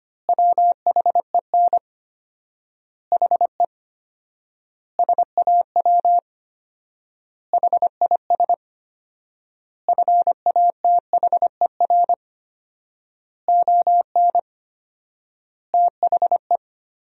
Morse code, heard as a keyed tone, about 25 wpm, 700 Hz. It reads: WHEN HE SAW HIS FATHER ON THE
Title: 4x Word-Spacing - Sentences from Top 500 Words - 25wpm